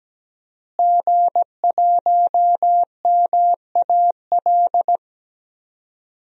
Morse code, heard as a keyed tone, about 17 wpm, 700 Hz. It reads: G1MAL
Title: DXCC-100 Call Signs - 5 Characters - Repeated 3x - 17wpm